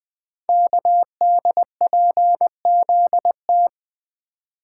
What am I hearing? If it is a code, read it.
Morse code, heard as a keyed tone, about 20 wpm, 700 Hz. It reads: KDPZT
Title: Code Groups - 5 Letters - 20wpm